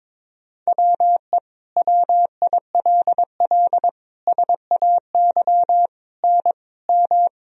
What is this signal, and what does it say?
Morse code, heard as a keyed tone, about 21 wpm, 700 Hz. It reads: WE WILL SAY N M
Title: Sentences from Top 200 Words and Abbreviations 22wpm